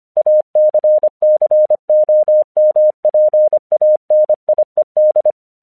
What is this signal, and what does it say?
Morse code, heard as a keyed tone, about 25 wpm, 610 Hz. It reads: ACCOMPANIED
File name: 1 English 25wpm / AQC05